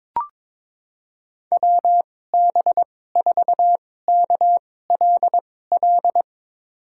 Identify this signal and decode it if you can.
Morse code, heard as a keyed tone, about 22 wpm, spaced as for 16 wpm, 700 Hz. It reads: WB4KLL
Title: USA Call Signs - 6 Characters - 2x Character Spacing - 22wpm